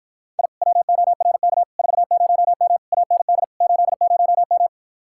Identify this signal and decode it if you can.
Morse code, heard as a keyed tone, about 44 wpm, 700 Hz. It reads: I WORK 40M AND 80M